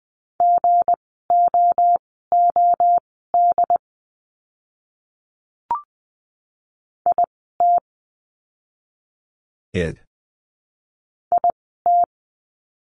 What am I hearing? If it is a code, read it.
Morse code, heard as a keyed tone, about 20 wpm, spaced as for 15 wpm, 700 Hz. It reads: GOOD IT IT
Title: 160 Lesson 26f - Letter g - Words 20wpm